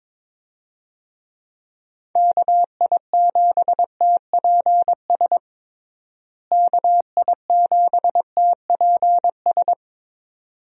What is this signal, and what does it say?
Morse code, heard as a keyed tone, about 22 wpm, 700 Hz. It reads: KI7TPS KI7TPS
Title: USA Call Signs - 6 Characters - Repeated 2x - 22wpm